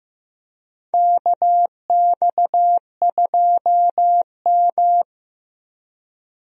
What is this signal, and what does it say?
Morse code, heard as a keyed tone, about 15 wpm, 700 Hz. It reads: KX2M
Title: USA Call Signs - 4 Characters - Repeated 3x - 15wpm